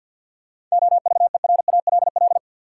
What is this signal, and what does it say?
Morse code, heard as a keyed tone, about 52 wpm, 700 Hz. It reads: OVERALL
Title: Sentences from Top 1000 Words - Review Part 7 - New Word Drill - 50wpm